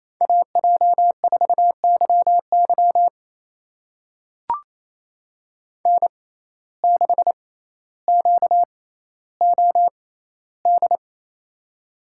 Morse code, heard as a keyed tone, about 28 wpm, spaced as for 14 wpm, 700 Hz. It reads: AJ4YY N6QOD
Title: USA Call Signs - 5 Characters - 6x Character Spacing - 28wpm